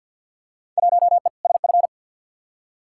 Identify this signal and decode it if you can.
Morse code, heard as a keyed tone, about 47 wpm, 700 Hz. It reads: 1E SF
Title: Field Day Exchanges - 50wpm